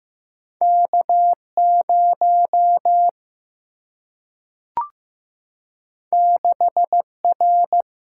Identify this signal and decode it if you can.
Morse code, heard as a keyed tone, about 15 wpm, 700 Hz. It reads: K0 6R
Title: Single-Letter Single-Number 15wpm